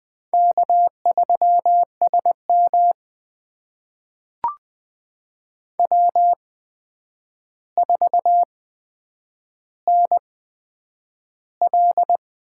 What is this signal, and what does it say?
Morse code, heard as a keyed tone, about 20 wpm, spaced as for 9 wpm, 700 Hz. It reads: K3SM W4NL
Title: USA Call Signs - 8x Character Spacing - 20wpm